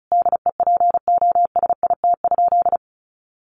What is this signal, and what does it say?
Morse code, heard as a keyed tone, about 34 wpm, 700 Hz. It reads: DEPOSIT?
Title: Sentences from Top 1300 Words 35wpm